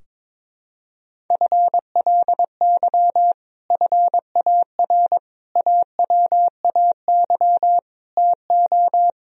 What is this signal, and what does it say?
Morse code, heard as a keyed tone, about 21 wpm, 700 Hz. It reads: FLY FAR AWAY TO
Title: Sentences from Top 1000 Words - Review Part 2 - 22wpm